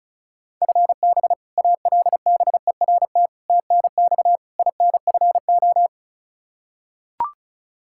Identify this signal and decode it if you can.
Morse code, heard as a keyed tone, about 36 wpm, 700 Hz. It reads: FB ALBERT TNX INFO
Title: Practice QSO - Exchange 3 - More Info - 35wpm